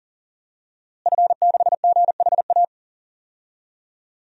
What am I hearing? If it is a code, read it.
Morse code, heard as a keyed tone, about 40 wpm, 700 Hz. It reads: F6GHA
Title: DXCC-100 Call Signs - 5 Characters - Repeated 2x - 40wpm